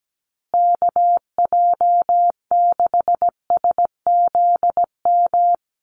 Morse code, heard as a keyed tone, about 17 wpm, 700 Hz. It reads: KJ6SZM
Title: USA Call Signs Repeated 2x 17wpm